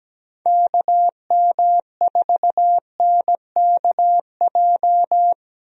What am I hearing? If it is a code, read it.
Morse code, heard as a keyed tone, about 17 wpm, 700 Hz. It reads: KM4NKJ
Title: USA Call Signs - 6 Characters - Repeated 2x - 17wpm